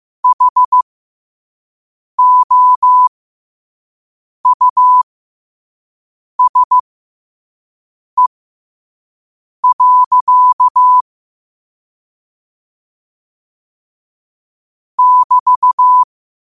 Morse code, heard as a keyed tone, about 15 wpm, spaced as for 5 wpm, 1000 Hz. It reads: HOUSE. <BT>
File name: qotdfiles / 5wpm